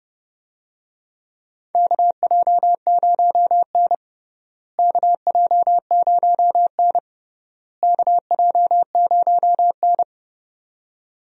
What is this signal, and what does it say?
Morse code, heard as a keyed tone, about 30 wpm, 700 Hz. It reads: KJ0N KJ0N KJ0N
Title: USA Call Signs - 4 Characters - Repeated 3x - 30wpm